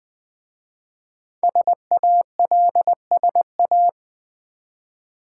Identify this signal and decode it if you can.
Morse code, heard as a keyed tone, about 20 wpm, 700 Hz. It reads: SALSA
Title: Easy Words with Hints - 20wpm